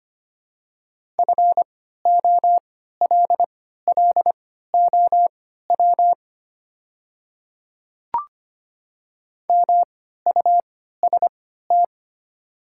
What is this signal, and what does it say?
Morse code, heard as a keyed tone, about 25 wpm, spaced as for 14 wpm, 700 Hz. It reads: FOLLOW MUST
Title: 3x Character Spacing - Top 200 Words - 25wpm